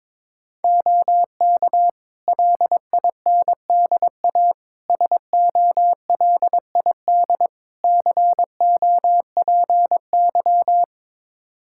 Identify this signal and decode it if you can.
Morse code, heard as a keyed tone, about 22 wpm, 700 Hz. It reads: OK LINDA SOLID COPY
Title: Practice QSO - Exchange 3 - More Info - 22wpm